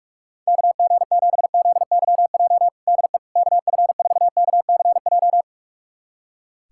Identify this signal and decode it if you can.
Morse code, heard as a keyed tone, about 42 wpm, 700 Hz. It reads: KG7ZYJ DE KF4KCJ
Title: USA Callsign DE Callsign - 6 Characters - 45wpm